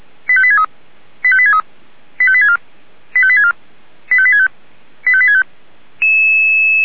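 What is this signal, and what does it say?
Five-tone selective call (ZVEI), heard as quick runs of five tones, tones of about 70 ms. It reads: 85863 85863 85864 85864 85865 85865
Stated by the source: ZVEI five-tone selective call received on SDR.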